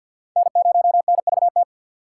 Morse code, heard as a keyed tone, about 47 wpm, 700 Hz. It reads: N0NVT
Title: USA Call Signs - 5 Characters - 4x Character Spacing - 50wpm